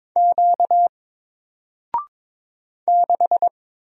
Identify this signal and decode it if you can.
Morse code, heard as a keyed tone, about 22 wpm, 700 Hz. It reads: Q 6
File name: Single-Letter-Number-22wpm